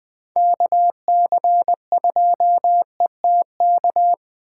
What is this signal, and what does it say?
Morse code, heard as a keyed tone, about 24 wpm, 700 Hz. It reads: KC2ETK
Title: Speed Racing - USA Call Signs 20wpm